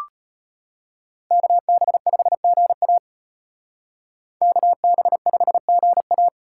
Morse code, heard as a keyed tone, about 32 wpm, 700 Hz. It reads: KB5GA KB5GA